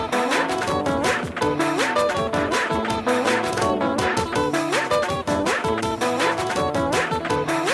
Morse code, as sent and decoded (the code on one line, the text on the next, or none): none